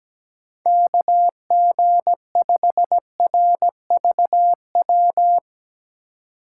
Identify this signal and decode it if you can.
Morse code, heard as a keyed tone, about 17 wpm, 700 Hz. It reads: KG5RVW
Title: USA Call Signs - 6 Characters - 3x Character Spacing - 17wpm